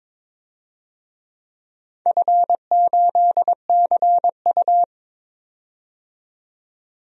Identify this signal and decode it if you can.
Morse code, heard as a keyed tone, about 22 wpm, 700 Hz. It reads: F8CU
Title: DXCC-100 Call Signs - 4 Characters - 22wpm